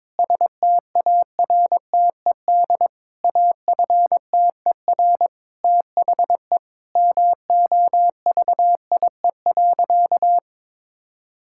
Morse code, heard as a keyed tone, about 22 wpm, 700 Hz. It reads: STARTED AFTER THE MOVIE.
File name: practice / 22wpm / Sentences with 2-Word Phrases - 22wpm